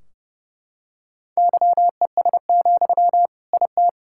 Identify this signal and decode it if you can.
Morse code, heard as a keyed tone, about 32 wpm, 700 Hz. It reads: YES, IT
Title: Sentences from Top 1600 Words 30wpm